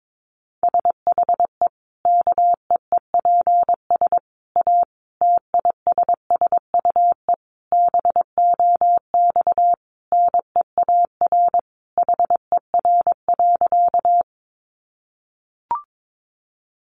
Morse code, heard as a keyed tone, about 22 wpm, 700 Hz. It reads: SHE KEEPS A TISSUE BOX NEAR HER.
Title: Easy Phrases 22wpm